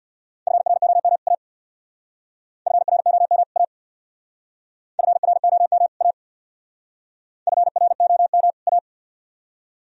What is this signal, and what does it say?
Morse code, heard as a keyed tone, about 62 wpm, 700 Hz. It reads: FROM A FROM A FROM A FROM A
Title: Speed-Racing - Sets of 2 Words 50wpm